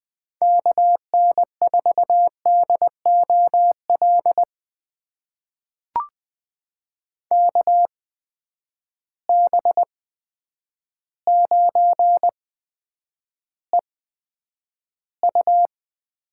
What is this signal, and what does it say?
Morse code, heard as a keyed tone, about 20 wpm, spaced as for 9 wpm, 700 Hz. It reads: KN4DOL KB9EU
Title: USA Call Signs - 6 Characters - 8x Character Spacing - 20wpm